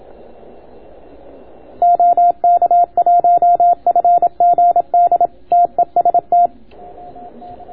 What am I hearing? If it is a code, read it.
Morse code, heard as a keyed tone, about 27 wpm, 690 Hz. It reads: OK1FGD TEST